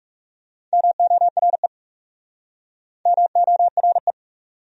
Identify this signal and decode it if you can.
Morse code, heard as a keyed tone, about 37 wpm, 700 Hz. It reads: MORE MORE
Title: Speed-Racing Top 100 Words 30wpm